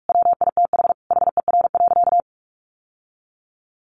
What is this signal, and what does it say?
Morse code, heard as a keyed tone, about 44 wpm, 700 Hz. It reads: WITH HER.
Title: Sentences from Top 1100 Words 45wpm